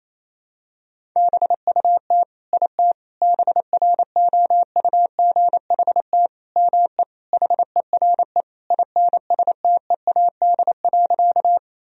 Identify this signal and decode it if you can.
Morse code, heard as a keyed tone, about 28 wpm, 700 Hz. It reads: BUT IT BROUGHT ME HERE INSTEAD.
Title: Sentences from Top 1000 Words - Review Part 4 - 28wpm